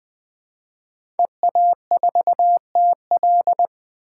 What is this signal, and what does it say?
Morse code, heard as a keyed tone, about 20 wpm, 700 Hz. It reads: EA4TL
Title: 2023 POTA Hunter Call Signs - 20wpm